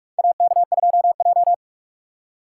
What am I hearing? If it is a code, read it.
Morse code, heard as a keyed tone, about 42 wpm, 700 Hz. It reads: AK2J
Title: USA Call Signs - 3x Character Spacing - 45wpm